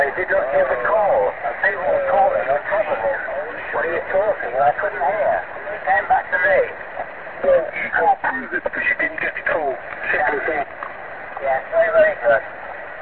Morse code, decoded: none